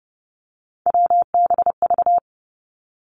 Morse code, heard as a keyed tone, about 30 wpm, 700 Hz. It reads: WBV
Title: Three Challenging Letter-Number Combinations 30wpm